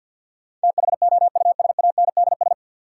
Morse code, heard as a keyed tone, about 49 wpm, 700 Hz. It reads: THOUSANDS
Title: Sets of 4 Words - New Word Drill - 50wpm